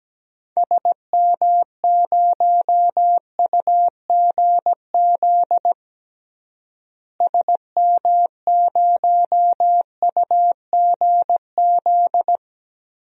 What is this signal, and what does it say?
Morse code, heard as a keyed tone, about 17 wpm, 700 Hz. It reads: SM0UGZ SM0UGZ
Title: DXCC-100 Call Signs - 6 Characters - Repeated 3x - 17wpm